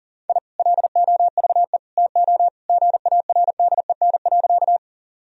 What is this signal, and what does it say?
Morse code, heard as a keyed tone, about 40 wpm, 700 Hz. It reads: I LOVE TO GARDEN.
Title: Sentences from Top 1000 Words - Review Part 4 - 40wpm